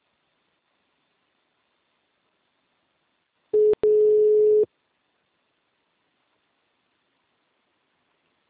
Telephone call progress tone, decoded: ringback tone

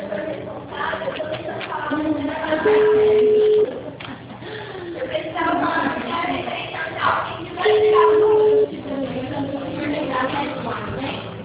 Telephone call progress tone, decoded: ringback tone

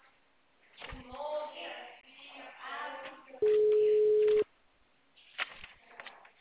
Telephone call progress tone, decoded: ringback tone

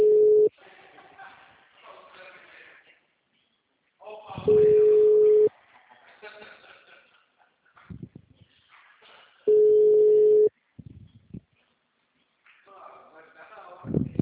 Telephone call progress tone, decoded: ringback tone

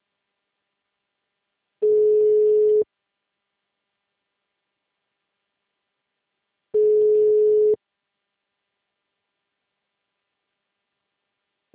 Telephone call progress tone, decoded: ringback tone

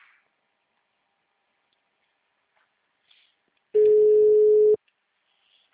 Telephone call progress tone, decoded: ringback tone

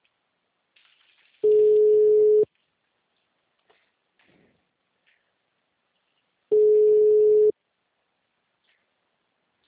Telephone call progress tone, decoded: ringback tone

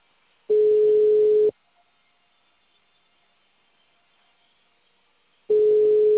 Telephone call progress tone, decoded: ringback tone

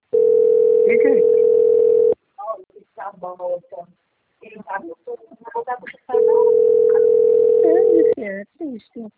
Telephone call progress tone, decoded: ringback tone

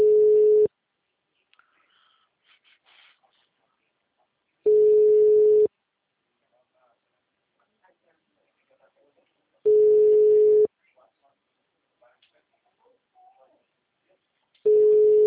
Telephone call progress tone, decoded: ringback tone

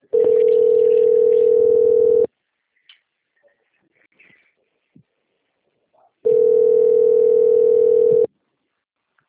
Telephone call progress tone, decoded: ringback tone

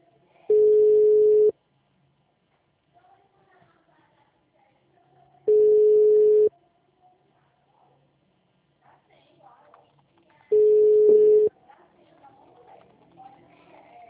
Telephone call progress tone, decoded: ringback tone

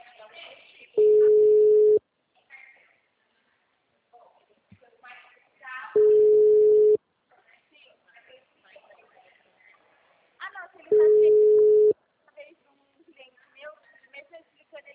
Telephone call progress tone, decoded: ringback tone

